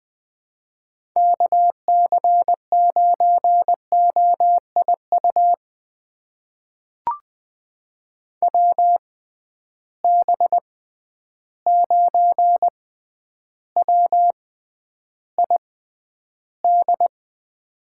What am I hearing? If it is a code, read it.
Morse code, heard as a keyed tone, about 20 wpm, spaced as for 10 wpm, 700 Hz. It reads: KC9OIU WB9WID
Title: USA Call Signs - 6 Characters - 6x Character Spacing - 20wpm